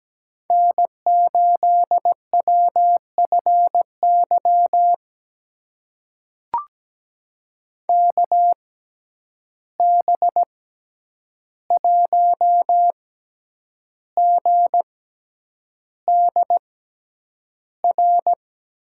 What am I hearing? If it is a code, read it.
Morse code, heard as a keyed tone, about 17 wpm, spaced as for 8 wpm, 700 Hz. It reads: N8WFY KB1GDR
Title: USA Call Signs - 6x Character Spacing - 17wpm